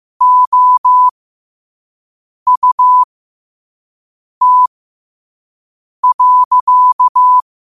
Morse code, heard as a keyed tone, about 15 wpm, spaced as for 5 wpm, 1000 Hz. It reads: OUT.
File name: qotdfiles / 5wpm